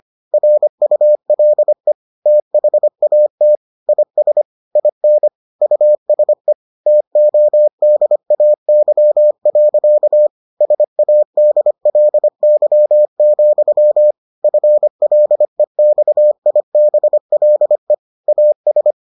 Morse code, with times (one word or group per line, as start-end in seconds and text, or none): 0.3-1.9 RULE
2.3-3.6 THAT
3.9-4.4 IS
4.8-5.3 IN
5.6-6.5 USE
6.9-10.3 TODAY.
10.6-14.1 SADLY,
14.4-17.9 FLEXIBLE
18.3-18.9 AS